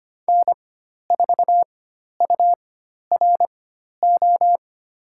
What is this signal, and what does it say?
Morse code, heard as a keyed tone, about 25 wpm, spaced as for 12 wpm, 700 Hz. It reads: N4URO
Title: USA Call Signs - 5 Characters - 4x Character Spacing - 25wpm